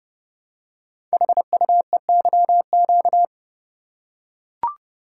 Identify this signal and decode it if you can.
Morse code, heard as a keyed tone, about 30 wpm, 700 Hz. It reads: HUEYQ
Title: Code Groups - 5 Letters - 30wpm